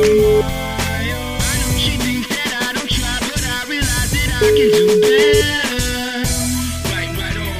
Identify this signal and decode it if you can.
Ringback tone (Europe, 425 Hz), with noise.